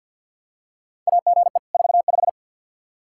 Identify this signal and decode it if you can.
Morse code, heard as a keyed tone, about 49 wpm, 700 Hz. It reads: AGE 45
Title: Practice QSO - Exchange 3 - More Info - 50wpm